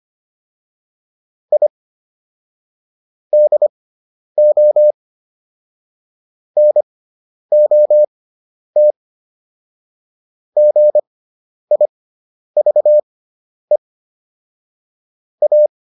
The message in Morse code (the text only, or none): I DO NOT GIVE A